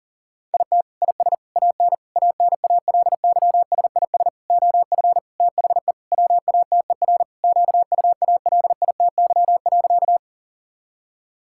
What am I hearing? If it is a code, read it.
Morse code, heard as a keyed tone, about 40 wpm, 700 Hz. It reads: IT IS AN ANALYSIS OF THE WATER QUALITY.
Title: Sentences from Top 1000 Words - Review Part 5 - 40wpm